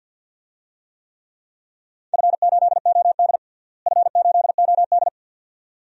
Morse code, heard as a keyed tone, about 52 wpm, 700 Hz. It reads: F8OD F8OD
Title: DXCC-100 Call Signs - 4 Characters - Repeated 2x - 50wpm